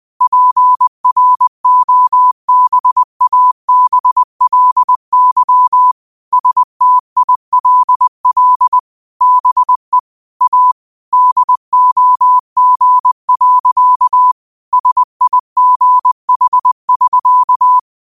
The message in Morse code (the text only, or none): PROBABLY STILL BE A DOG. SIGH<SK>